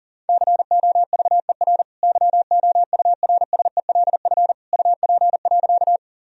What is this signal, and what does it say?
Morse code, heard as a keyed tone, about 40 wpm, 700 Hz. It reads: COVER YOURSELF UP.
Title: Sentences from Top 1000 Words - Review Part 1 - 40wpm